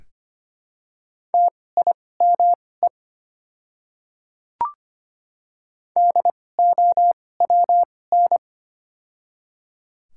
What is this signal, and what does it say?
Morse code, heard as a keyed tone, about 25 wpm, spaced as for 18 wpm, 700 Hz. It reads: TIME DOWN
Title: 2x Character Spacing - Top 100 Words - 25wpm